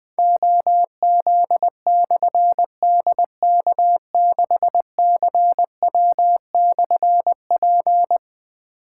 Morse code, heard as a keyed tone, about 20 wpm, 700 Hz. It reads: OZ/DK6CW/P
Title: Difficult Call Signs 20wpm